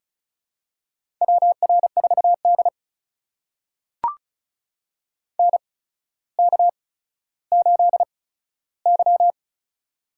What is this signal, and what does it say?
Morse code, heard as a keyed tone, about 36 wpm, spaced as for 15 wpm, 700 Hz. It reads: WR4D NK8Y
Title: USA Call Signs - 4 Characters - 8x Character Spacing - 35wpm